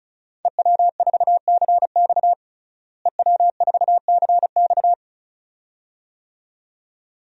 Morse code, heard as a keyed tone, about 36 wpm, 700 Hz. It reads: EW4CX EW4CX
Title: DXCC-100 Call Signs - 5 Characters - Repeated 2x - 35wpm